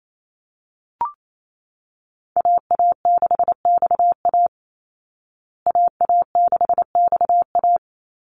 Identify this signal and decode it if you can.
Morse code, heard as a keyed tone, about 28 wpm, 700 Hz. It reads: AA6XA AA6XA
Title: Call Signs Repeated 2x - 28wpm